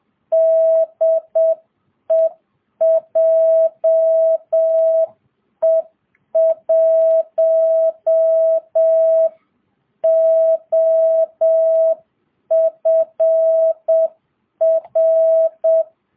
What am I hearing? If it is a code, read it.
Morse code, heard as a keyed tone, about 7 wpm, 660 Hz. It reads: DE JE1OFR